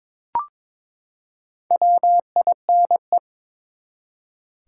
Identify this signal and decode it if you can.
Morse code, heard as a keyed tone, about 23 wpm, 700 Hz. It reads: WINE
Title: Four Letter Words - 22wpm